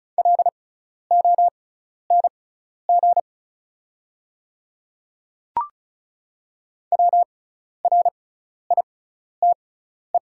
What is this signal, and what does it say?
Morse code, heard as a keyed tone, about 36 wpm, spaced as for 12 wpm, 690 Hz. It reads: LONG WRITE
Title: 6x Character Spacing - Top 100 Words - 35wpm